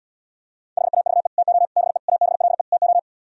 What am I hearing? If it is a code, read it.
Morse code, heard as a keyed tone, about 77 wpm, 700 Hz. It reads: HAVE TO BE ABLE TO